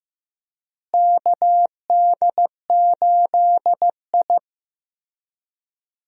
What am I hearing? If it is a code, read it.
Morse code, heard as a keyed tone, about 15 wpm, 700 Hz. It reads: KD8I
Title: USA Call Signs - 4 Characters - Repeated 2x - 15wpm